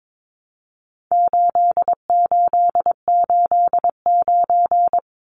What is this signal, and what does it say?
Morse code, heard as a keyed tone, about 22 wpm, 700 Hz. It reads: 8889
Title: Numbers - 4-Digit 22wpm